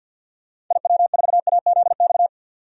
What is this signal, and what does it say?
Morse code, heard as a keyed tone, about 52 wpm, 700 Hz. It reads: IW4AZX
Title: DXCC-100 Call Signs - 6 Characters - 50wpm